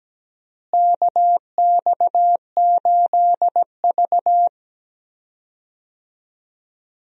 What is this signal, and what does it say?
Morse code, heard as a keyed tone, about 17 wpm, 700 Hz. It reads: KX8V
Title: USA Call Signs - 4 Characters - Repeated 2x - 17wpm